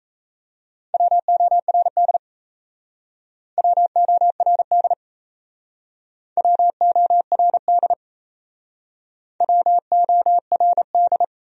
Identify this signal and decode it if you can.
Morse code, heard as a keyed tone, about 36 wpm, 690 Hz. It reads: WORD WORD WORD WORD